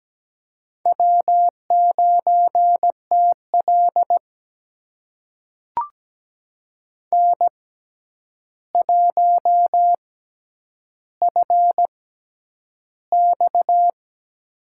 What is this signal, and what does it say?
Morse code, heard as a keyed tone, about 17 wpm, spaced as for 9 wpm, 700 Hz. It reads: W9TL N1FX
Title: USA Call Signs - 4 Characters - 6x Character Spacing - 17wpm